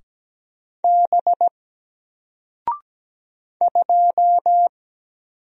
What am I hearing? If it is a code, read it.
Morse code, heard as a keyed tone, about 17 wpm, 700 Hz. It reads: B 2
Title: Single Character (Letter:Number:Punctuation:Prosign) 17wpm